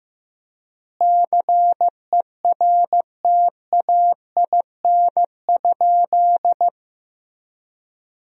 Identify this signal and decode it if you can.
Morse code, heard as a keyed tone, about 15 wpm, 700 Hz. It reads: CERTAIN?
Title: 4x Word-Spacing - Sentences from Top 500 Words - 15wpm